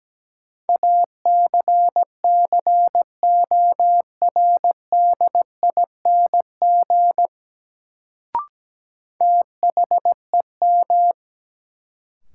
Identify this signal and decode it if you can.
Morse code, heard as a keyed tone, about 17 wpm, 700 Hz. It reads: ACCORDING THEM
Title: Sets of 2 Words - New Word Drill - 17wpm